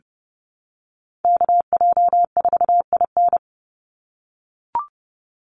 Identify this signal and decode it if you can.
Morse code, heard as a keyed tone, about 30 wpm, 700 Hz. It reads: KJ4IN
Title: USA Call Signs 30wpm